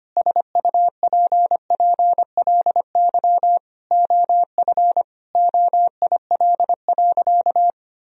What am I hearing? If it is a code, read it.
Morse code, heard as a keyed tone, about 25 wpm, 700 Hz. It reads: SUPPLY OF OIL.
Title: Sentences from Top 1000 Words - Review Part 4 - 25wpm